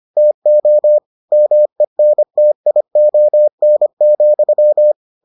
Morse code, heard as a keyed tone, about 25 wpm, 600 Hz. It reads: TO MENTION,